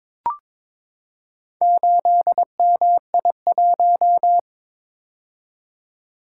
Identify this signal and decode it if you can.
Morse code, heard as a keyed tone, about 22 wpm, 700 Hz. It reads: 8MI1
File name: practice / 22wpm / Four Letter-Number Combinations - 22wpm